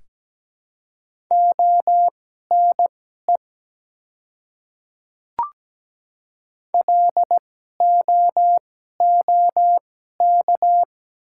Morse code, heard as a keyed tone, about 17 wpm, spaced as for 12 wpm, 700 Hz. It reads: ONE LOOK
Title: 2x Character Spacing - Top 300 Words - 17wpm